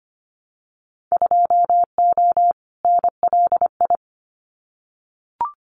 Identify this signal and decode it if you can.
Morse code, heard as a keyed tone, about 25 wpm, 700 Hz. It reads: 2O NLI
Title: Winter Field Day Exchanges 25wpm